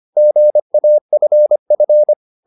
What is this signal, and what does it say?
Morse code, heard as a keyed tone, about 25 wpm, 600 Hz. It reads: GAFF
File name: the old man and the sea part 1 25wpm